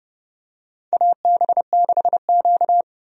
Morse code, heard as a keyed tone, about 30 wpm, 700 Hz. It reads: AB6Q
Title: USA Call Signs - 4 Characters - 30wpm